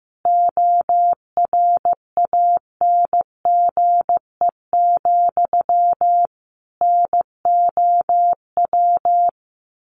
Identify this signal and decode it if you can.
Morse code, heard as a keyed tone, about 15 wpm, 700 Hz. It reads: ORANGE, NOW
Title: A Sound Of Thunder 15wpm